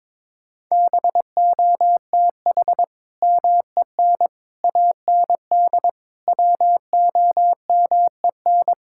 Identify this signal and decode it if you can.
Morse code, heard as a keyed tone, about 22 wpm, 700 Hz. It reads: BOTH MEN AND WOMEN
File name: practice / 22wpm / Sentences with 3-Word Phrases - 22wpm